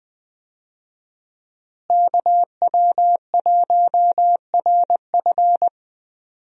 Morse code, heard as a keyed tone, about 21 wpm, 700 Hz. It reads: KW1RF
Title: USA Call Signs - 5 Characters - Repeated 2x - 20wpm